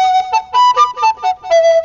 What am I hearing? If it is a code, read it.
There is no code.